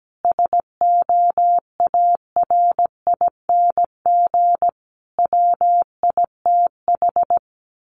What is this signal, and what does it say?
Morse code, heard as a keyed tone, about 17 wpm, 700 Hz. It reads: SOARING WITH
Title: A Sound Of Thunder 17wpm